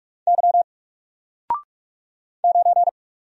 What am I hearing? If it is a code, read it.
Morse code, heard as a keyed tone, about 44 wpm, 700 Hz. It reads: Y 9
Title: Single Character (Letter:Number:Punctuation:Prosign) 45wpm